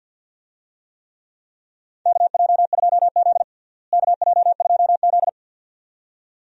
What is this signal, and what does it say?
Morse code, heard as a keyed tone, about 48 wpm, 700 Hz. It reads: KJ2Z KJ2Z
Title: USA Call Signs - 4 Characters - Repeated 2x - 50wpm